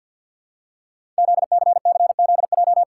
Morse code, heard as a keyed tone, about 46 wpm, 700 Hz. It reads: ZCCZJ
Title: Code Groups - 5 Letters - 50wpm